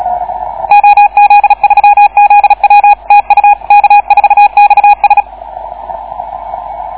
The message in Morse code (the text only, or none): OZ3ZW TU K4XS